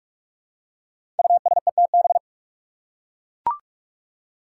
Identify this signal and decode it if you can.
Morse code, heard as a keyed tone, about 44 wpm, 700 Hz. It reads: USETB
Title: Five Challenging Letter-Number Combinations 45wpm